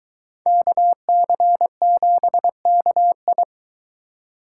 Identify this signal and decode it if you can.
Morse code, heard as a keyed tone, about 23 wpm, 700 Hz. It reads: KC7KI